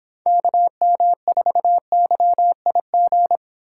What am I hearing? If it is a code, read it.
Morse code, heard as a keyed tone, about 31 wpm, 700 Hz. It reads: KM4YIG